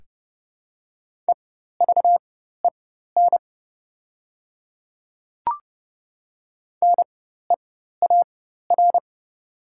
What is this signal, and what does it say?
Morse code, heard as a keyed tone, about 30 wpm, spaced as for 14 wpm, 700 Hz. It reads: EVEN NEAR
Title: 4x Character Spacing - Top 300 Words - 30wpm